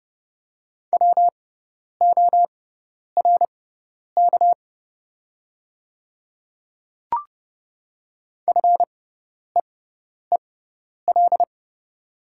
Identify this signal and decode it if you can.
Morse code, heard as a keyed tone, about 30 wpm, spaced as for 10 wpm, 700 Hz. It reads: WORK FEEL
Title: 6x Character Spacing - Top 500 Words - 30wpm